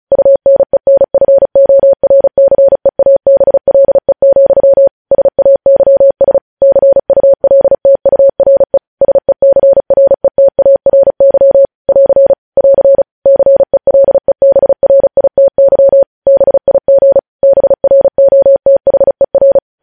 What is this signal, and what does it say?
Morse code, heard as a keyed tone, about 36 wpm, 550 Hz. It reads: UNENFORCEABLE, SAYS CULTURE SECRETARY <AR> <AR> CELEBRITY BIG BROTHER
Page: News headlines in Morse code at 35 wpm